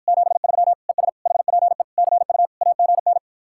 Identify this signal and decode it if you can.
Morse code, heard as a keyed tone, about 54 wpm, 700 Hz. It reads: 73 ES HPE CU AGN